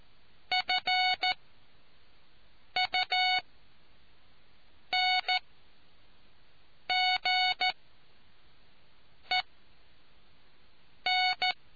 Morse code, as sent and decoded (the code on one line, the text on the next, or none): ..-. ..- -. --. -.
FUNGEN